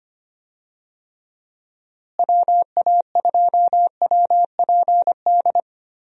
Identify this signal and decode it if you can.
Morse code, heard as a keyed tone, about 25 wpm, 700 Hz. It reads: WA2WPD
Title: USA Call Signs - 6 Characters - Repeated 2x - 25wpm